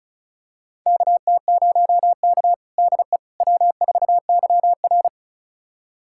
Morse code, heard as a keyed tone, about 36 wpm, 700 Hz. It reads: KT0K DE W4YR
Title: USA Callsign DE Callsign - 4 Characters - 35wpm